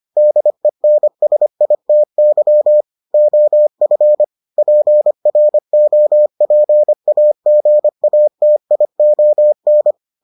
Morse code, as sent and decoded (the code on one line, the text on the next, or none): -.. -. ... .. - -.-- / --- ..-. / .--. .-. --- .--. .- --. .- - .. --- -.
DENSITY OF PROPAGATION